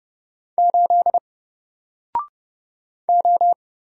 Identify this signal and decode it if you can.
Morse code, heard as a keyed tone, about 30 wpm, 700 Hz. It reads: 8 O
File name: Single-Letter-Number-30wpm